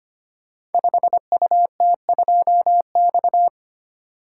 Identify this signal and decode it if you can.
Morse code, heard as a keyed tone, about 25 wpm, 700 Hz. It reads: 5UT2X
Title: Code Groups - 5 Letter:Numbers - 25wpm